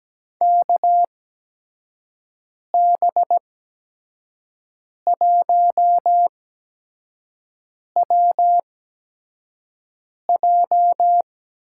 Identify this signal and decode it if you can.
Morse code, heard as a keyed tone, about 17 wpm, spaced as for 5 wpm, 700 Hz. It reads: KB1WJ
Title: USA Call Signs - 6 Characters - 8x Character Spacing - 17wpm